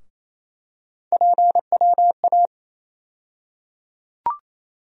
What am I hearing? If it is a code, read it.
Morse code, heard as a keyed tone, about 28 wpm, 700 Hz. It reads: PWA
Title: USA Call Sign Postfixes - 3 Letters 28wpm